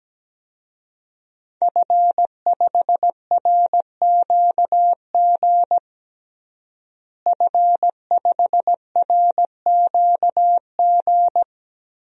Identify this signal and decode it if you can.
Morse code, heard as a keyed tone, about 17 wpm, 700 Hz. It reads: F5RQG F5RQG
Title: DXCC-100 Call Signs - 5 Characters - Repeated 3x - 17wpm